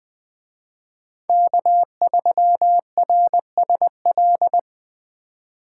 Morse code, heard as a keyed tone, about 23 wpm, 700 Hz. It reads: K3RSL
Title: Speed Racing - USA 1x3 Call Signs 20wpm